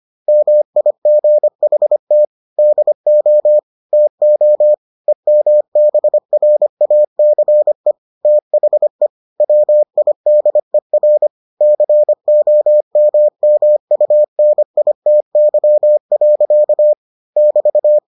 Morse code, heard as a keyed tone, about 25 wpm, 600 Hz. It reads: MIGHT DO TO EMBRACE THE WIDER COMMUNITY. <BT>